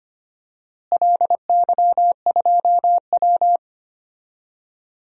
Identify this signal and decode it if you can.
Morse code, heard as a keyed tone, about 25 wpm, 700 Hz. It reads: LY2W
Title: DXCC-100 Call Signs - 4 Characters - Repeated 2x - 25wpm